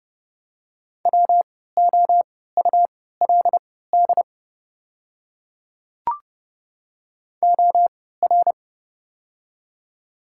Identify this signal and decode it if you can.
Morse code, heard as a keyed tone, about 32 wpm, spaced as for 17 wpm, 700 Hz. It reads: WOULD OR
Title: 3x Character Spacing - Top 100 Words - 30wpm